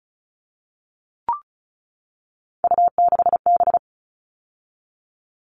Morse code, heard as a keyed tone, about 34 wpm, 700 Hz. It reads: U6B
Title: All Cut Numbers - 3-Digit 35wpm